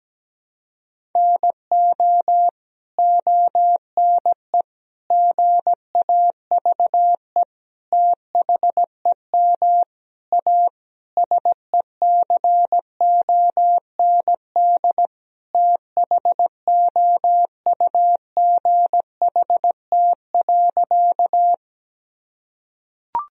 Morse code, heard as a keyed tone, about 17 wpm, 700 Hz. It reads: NO ONE GAVE THEM A SECOND THOUGHT.
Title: Sentences from Top 1000 Words - Review Part 1 - 17wpm